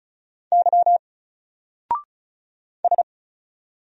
Morse code, heard as a keyed tone, about 34 wpm, 700 Hz. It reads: Y S
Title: Single Letter-Number 35wpm